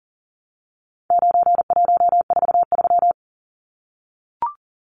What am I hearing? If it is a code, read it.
Morse code, heard as a keyed tone, about 40 wpm, 700 Hz. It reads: 9143
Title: Numbers - 4-Digit 40wpm